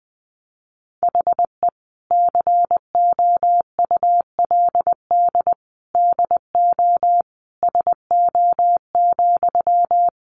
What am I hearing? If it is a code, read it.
Morse code, heard as a keyed tone, about 19 wpm, 700 Hz. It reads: HE COULD DO SO,
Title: Sentences from Top 1500 Words 20wpm